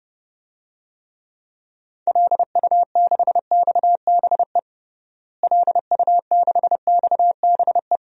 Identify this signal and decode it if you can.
Morse code, heard as a keyed tone, about 30 wpm, 700 Hz. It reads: LU6XBE LU6XBE
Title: DXCC-100 Call Signs - 6 Characters - Repeated 2x - 30wpm